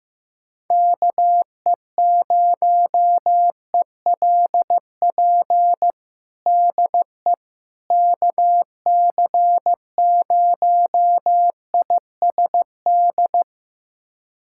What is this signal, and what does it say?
Morse code, heard as a keyed tone, about 16 wpm, 700 Hz. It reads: KE0ELP DE KC0ISD
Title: USA Callsign DE Callsign - 6 Characters - 15wpm